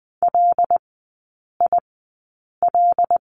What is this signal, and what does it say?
Morse code, heard as a keyed tone, about 20 wpm, spaced as for 8 wpm, 700 Hz. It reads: LIL
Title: 074 Lesson 13e - Letter L - Three Characters 20wpm